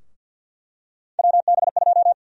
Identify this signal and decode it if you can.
Morse code, heard as a keyed tone, about 48 wpm, 700 Hz. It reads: WB2
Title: USA Call Sign Prefixes - 50wpm